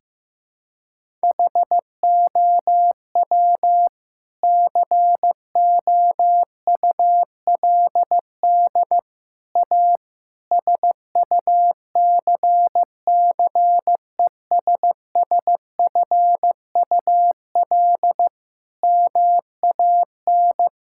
Morse code, heard as a keyed tone, about 15 wpm, 700 Hz. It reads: HOW COULD A SUCCESSFUL MAN
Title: Sentences from Top 1000 Words - Review Part 6 - 15wpm